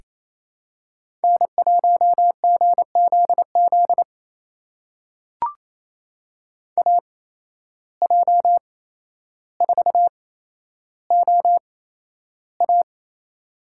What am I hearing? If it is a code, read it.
Morse code, heard as a keyed tone, about 28 wpm, spaced as for 12 wpm, 700 Hz. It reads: N1GZZ AJ4OA
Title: USA Call Signs - 8x Character Spacing - 28wpm